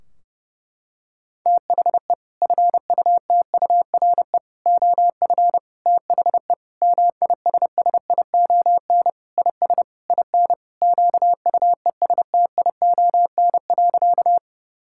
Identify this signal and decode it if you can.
Morse code, heard as a keyed tone, about 30 wpm, 700 Hz. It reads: THE FUTURE OF THE MISSION IS IN QUESTION.
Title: Sentences from Top 1600 Words 30wpm